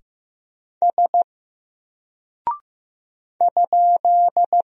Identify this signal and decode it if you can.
Morse code, heard as a keyed tone, about 15 wpm, 700 Hz. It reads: S ?
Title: Single Character (Letter:Number:Punctuation:Prosign) 15wpm